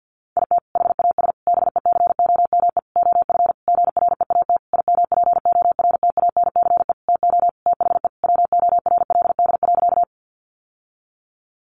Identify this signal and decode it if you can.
Morse code, heard as a keyed tone, about 50 wpm, 700 Hz. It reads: IT HAS BECOME OF GREAT IMPORTANCE TO THE WORLD.
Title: Sentences from Top 1200 Words 50wpm